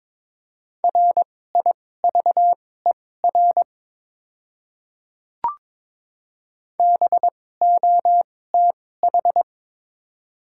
Morse code, heard as a keyed tone, about 22 wpm, spaced as for 16 wpm, 700 Hz. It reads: RIVER BOTH
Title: 2x Character Spacing - Top 300 Words - 22wpm